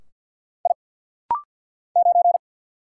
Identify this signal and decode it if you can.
Morse code, heard as a keyed tone, about 50 wpm, 690 Hz. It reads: I 9
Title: Single Letter-Number - Rapid-Fire 50wpm